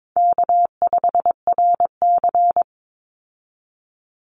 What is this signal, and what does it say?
Morse code, heard as a keyed tone, about 22 wpm, 700 Hz. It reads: K5RC
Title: Easy Call Signs Repeated 2x - 22wpm